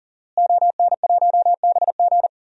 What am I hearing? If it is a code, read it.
Morse code, heard as a keyed tone, about 37 wpm, 700 Hz. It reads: ON1BG
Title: DXCC-100 Call Signs - 5 Characters - Repeated 3x - 40wpm